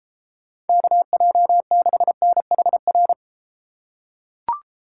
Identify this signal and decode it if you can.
Morse code, heard as a keyed tone, about 32 wpm, 700 Hz. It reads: KJ6NHR